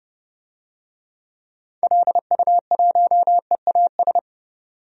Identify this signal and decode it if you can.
Morse code, heard as a keyed tone, about 30 wpm, 700 Hz. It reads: LU1EAS
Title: DXCC-100 Call Signs - 6 Characters - 30wpm